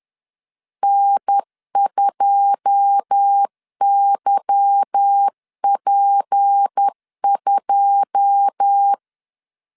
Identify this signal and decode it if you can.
Morse code, heard as a keyed tone, about 10 wpm, 800 Hz. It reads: N2YP2
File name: LEZIONI / 9 2 10wpm 2736